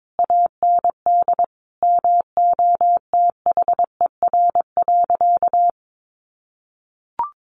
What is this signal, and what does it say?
Morse code, heard as a keyed tone, about 22 wpm, 700 Hz. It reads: AND MOTHER.
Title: Sentences from Top 1100 Words 22wpm